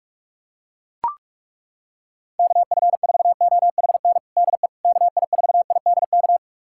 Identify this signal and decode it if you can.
Morse code, heard as a keyed tone, about 41 wpm, 700 Hz. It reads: KF4OHN DE KI4IDK
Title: USA Callsign DE Callsign - 6 Characters - 45wpm